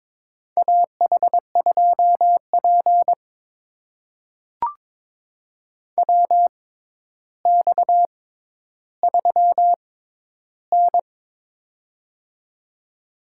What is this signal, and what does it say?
Morse code, heard as a keyed tone, about 22 wpm, spaced as for 11 wpm, 700 Hz. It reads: AH2P WX3N
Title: USA Call Signs - 6x Character Spacing - 22wpm